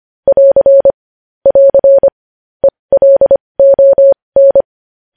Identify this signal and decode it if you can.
Morse code, heard as a keyed tone, about 25 wpm, spaced as for 20 wpm, 550 Hz. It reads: <AR> <AR> ELON
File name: podcast / News headlines-1x-25x20